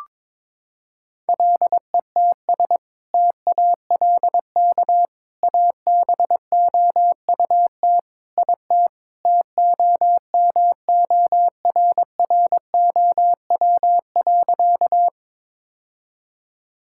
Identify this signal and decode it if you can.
Morse code, heard as a keyed tone, about 22 wpm, 700 Hz. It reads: LETS TALK ABOUT IT TOMORROW.
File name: practice / 22wpm / Sentences with 2-Word Phrases - 22wpm